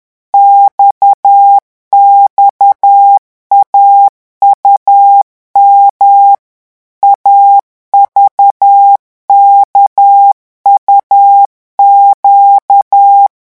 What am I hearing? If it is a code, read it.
Morse code, heard as a keyed tone, about 10 wpm, 800 Hz. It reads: XXAUM AVKUQ
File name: morse mp3 / revision2-vitesse 10 mots